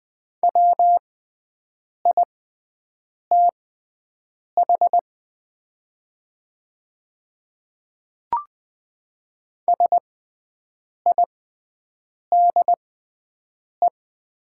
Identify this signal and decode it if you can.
Morse code, heard as a keyed tone, about 20 wpm, spaced as for 7 wpm, 700 Hz. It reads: WITH SIDE
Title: 6x Character Spacing - Top 100 Words - 20wpm